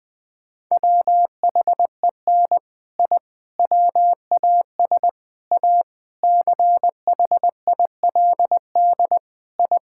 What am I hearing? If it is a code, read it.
Morse code, heard as a keyed tone, about 20 wpm, 700 Hz. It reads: WHEN I WAS A CHILD I
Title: Sentences with 3-Word Phrases - 20wpm